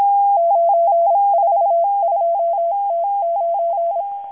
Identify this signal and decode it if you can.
Morse code, heard as a keyed tone, about 26 wpm, 680 Hz. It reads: 942T9
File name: HM02- -0437utc-7351khz